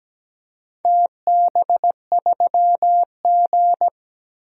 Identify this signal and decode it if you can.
Morse code, heard as a keyed tone, about 17 wpm, 700 Hz. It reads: TB3G
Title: DXCC-100 Call Signs - 4 Characters - 17wpm